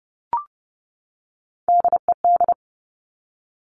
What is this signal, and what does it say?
Morse code, heard as a keyed tone, about 30 wpm, 700 Hz. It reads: DED
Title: Top Three Letter Combinations in Words 30wpm